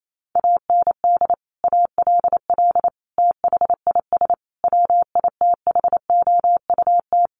Morse code, heard as a keyed tone, about 28 wpm, 700 Hz. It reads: AND ALL THIS WITHOUT
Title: Sentences from Top 1200 Words 28wpm